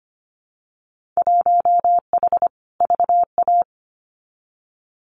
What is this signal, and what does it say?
Morse code, heard as a keyed tone, about 25 wpm, 690 Hz. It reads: 1H VA
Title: Winter Field Day Exchanges 25wpm